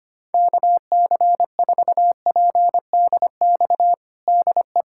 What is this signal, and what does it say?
Morse code, heard as a keyed tone, about 25 wpm, 700 Hz. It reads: KC4PDX DE
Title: USA Callsign DE Callsign 25wpm